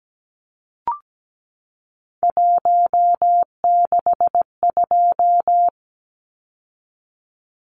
Morse code, heard as a keyed tone, about 17 wpm, 700 Hz. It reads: 162
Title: Numbers - 3-Digit 17wpm